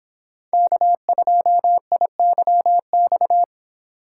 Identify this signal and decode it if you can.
Morse code, heard as a keyed tone, about 26 wpm, 700 Hz. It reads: K2IYX